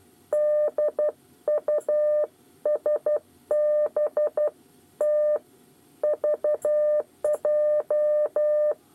Morse code, heard as a keyed tone, about 12 wpm, 600 Hz. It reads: DUSB T VJ
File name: 8 bis lezione cw 2